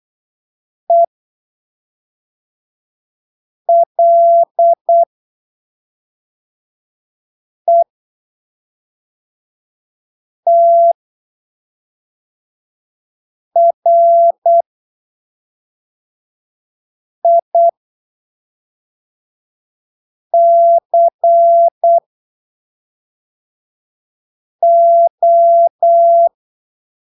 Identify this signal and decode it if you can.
Morse code, heard as a keyed tone, about 8 wpm, spaced as for 3 wpm, 680 Hz. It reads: ELETRICO
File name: arquivos / texto / texto 12 08ppm